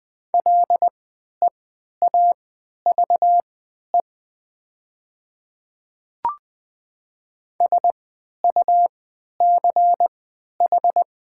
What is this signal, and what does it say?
Morse code, heard as a keyed tone, about 20 wpm, spaced as for 11 wpm, 700 Hz. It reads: LEAVE SUCH
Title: 3x Character Spacing - Top 500 Words - 20wpm